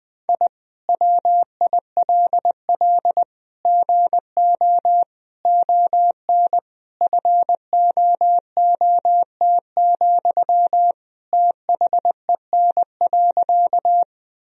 Morse code, heard as a keyed tone, about 20 wpm, 700 Hz. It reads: I WILL GO ON FOOT, THEN.
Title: Sentences from Top 1000 Words - Review Part 3 - 20wpm